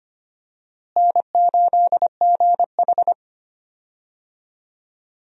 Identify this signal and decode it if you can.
Morse code, heard as a keyed tone, about 25 wpm, 700 Hz. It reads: N8GH
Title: USA Call Signs - 4 Characters - Repeated 3x - 25wpm